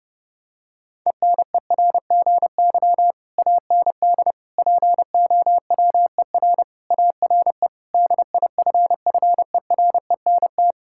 Morse code, heard as a keyed tone, about 30 wpm, 700 Hz. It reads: ENERGY AND POWER ARE DIFFERENT
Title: Sentences from Top 1000 Words - Review Part 6 - 30wpm